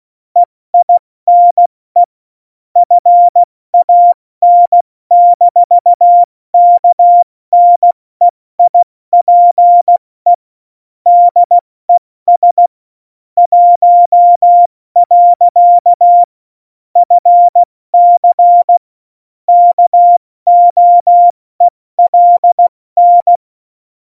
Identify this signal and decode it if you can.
Morse code, heard as a keyed tone, about 16 wpm, 700 Hz. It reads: EINE FAN-KNEIPE DES 1. FC KOELN
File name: Morsekurs / Witze / Witz 22. Dezember 2020, 70 F80